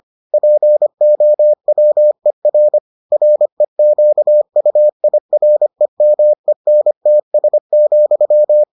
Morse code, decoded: POWER REQUIREMENTS,

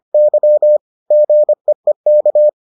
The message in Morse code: Y GEEK

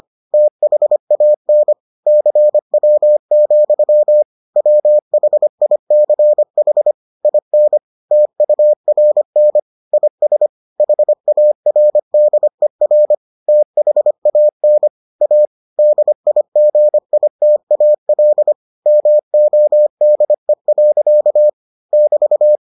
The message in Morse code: THAN CW, WHICH IN TURN IS HARDER THAN A DIGITAL MODE. <BT>

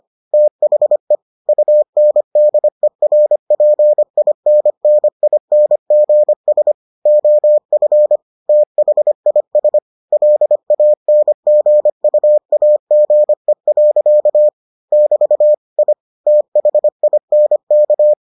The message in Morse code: THE UNDERPINNINGS OF THIS LANGUAGE. <BT> I THINK

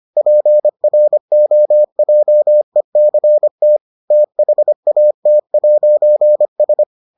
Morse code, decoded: PROJECT THAT'S

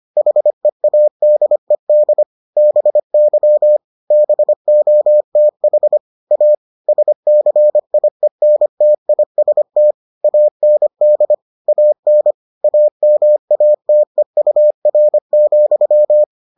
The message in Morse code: HEADED BY BOTH A SCIENTIST AND AN AMATEUR,

